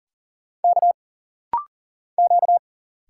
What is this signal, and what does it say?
Morse code, heard as a keyed tone, about 40 wpm, 700 Hz. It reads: K Q
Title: Single Letter-Number - Rapid-Fire 40wpm